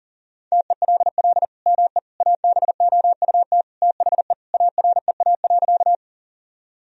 Morse code, heard as a keyed tone, about 40 wpm, 700 Hz. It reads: TELL ME ABOUT THE AREA.
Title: Sentences from Top 1000 Words - Review Part 2 - 40wpm